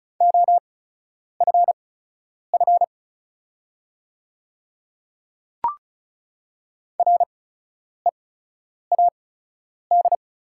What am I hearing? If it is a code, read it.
Morse code, heard as a keyed tone, about 36 wpm, spaced as for 10 wpm, 690 Hz. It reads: OFF READ
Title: 8x Character Spacing - Top 200 Words - 35wpm